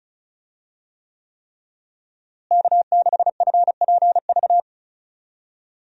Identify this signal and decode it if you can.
Morse code, heard as a keyed tone, about 34 wpm, 700 Hz. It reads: K6FPV
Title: USA Call Signs - 5 Characters - Repeated 3x - 35wpm